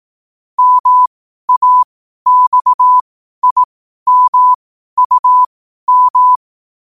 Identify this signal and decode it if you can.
Morse code, heard as a keyed tone, about 18 wpm, spaced as for 12 wpm, 1000 Hz. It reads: MAXIMUM
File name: qotdfiles / 12wpm